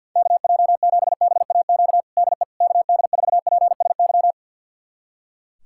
Morse code, heard as a keyed tone, about 51 wpm, 700 Hz. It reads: KJ7BAQ DE KD4PSY
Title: USA Callsign DE Callsign 50wpm